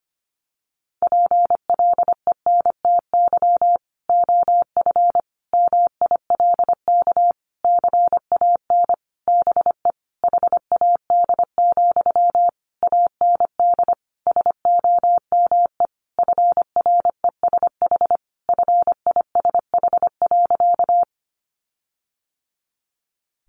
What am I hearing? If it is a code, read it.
Morse code, heard as a keyed tone, about 25 wpm, 700 Hz. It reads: PLENTY OF MILK CAN BE HAD, AND SOME FRESH FISH.
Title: Sentences from Top 1500 Words 25wpm